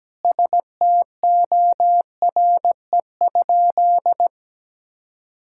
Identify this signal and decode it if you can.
Morse code, heard as a keyed tone, about 17 wpm, 700 Hz. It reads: STORE?
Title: Sentences from Top 1000 Words - Review Part 5 - 17wpm